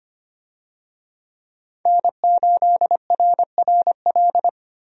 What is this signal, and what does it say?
Morse code, heard as a keyed tone, about 25 wpm, 700 Hz. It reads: N8RRL
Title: USA Call Signs - 5 Characters - Repeated 2x - 25wpm